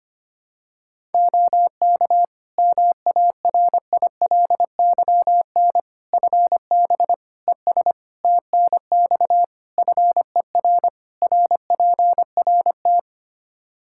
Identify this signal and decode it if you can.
Morse code, heard as a keyed tone, about 25 wpm, 700 Hz. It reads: OK MARILYN FB ES TNX FER RPRT
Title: Practice QSO - Exchange 2 - Info - 25wpm